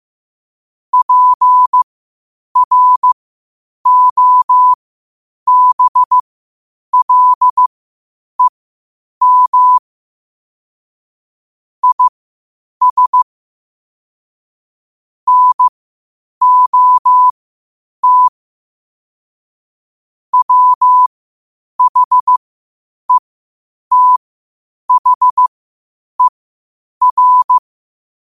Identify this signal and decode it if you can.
Morse code, heard as a keyed tone, about 15 wpm, spaced as for 8 wpm, 1000 Hz. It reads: PROBLEM IS NOT WHETHER
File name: qotdfiles / 8wpm